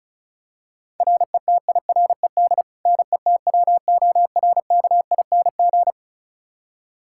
Morse code, heard as a keyed tone, about 34 wpm, 700 Hz. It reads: RETIRED NETWORKING
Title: K1USN SST QSO Practice 35wpm